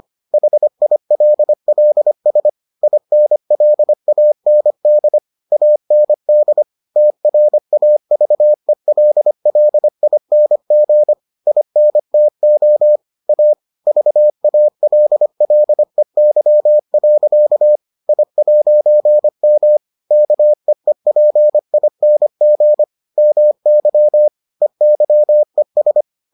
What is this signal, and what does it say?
Morse code, heard as a keyed tone, about 25 wpm, 600 Hz. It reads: HILLS INLAND AND TRAVELLING INTO A VALLEY. I'M KEEPING MY EYES